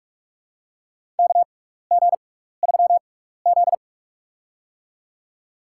Morse code, heard as a keyed tone, about 47 wpm, spaced as for 16 wpm, 700 Hz. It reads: KG3Z
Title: USA Call Signs - 6x Character Spacing - 45wpm